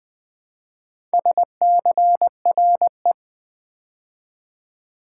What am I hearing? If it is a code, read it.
Morse code, heard as a keyed tone, about 20 wpm, 700 Hz. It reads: SCRE
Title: Build A Word - 2 Letters At A Time - 6-Letter Words - 20wpm